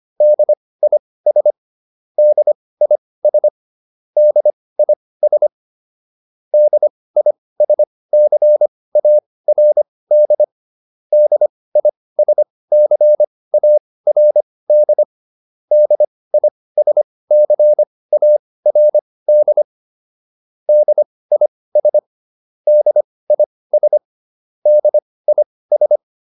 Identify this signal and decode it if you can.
Morse code, heard as a keyed tone, about 25 wpm, spaced as for 18 wpm, 600 Hz. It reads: DIS DIS DIS DISCARD DISCARD DISCARD DIS DIS DIS